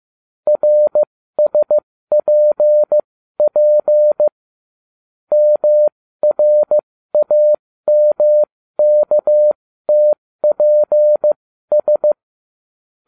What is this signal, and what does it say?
Morse code, heard as a keyed tone, about 15 wpm, spaced as for 13 wpm, 600 Hz. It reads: RSPP MRAMKTPS